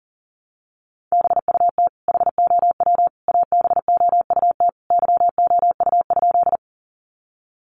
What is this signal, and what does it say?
Morse code, heard as a keyed tone, about 40 wpm, 700 Hz. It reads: BUT HOW ABOUT YOU?
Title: Sentences from Top 100 Words 40wpm